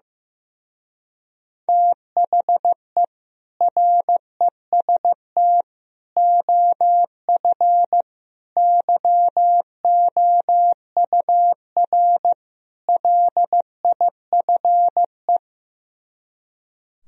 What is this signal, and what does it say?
Morse code, heard as a keyed tone, about 15 wpm, 700 Hz. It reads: THE REST OF YOUR LIFE
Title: Speed-Racing - Sets of 5 Words 15wpm